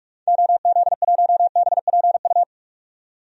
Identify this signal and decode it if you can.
Morse code, heard as a keyed tone, about 46 wpm, 700 Hz. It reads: OZ1BPU
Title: DXCC-100 Call Signs - 6 Characters - 45wpm